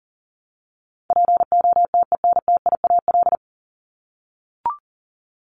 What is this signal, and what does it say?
Morse code, heard as a keyed tone, about 40 wpm, 700 Hz. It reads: POTENTIAL
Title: Top 501-750 Words 40wpm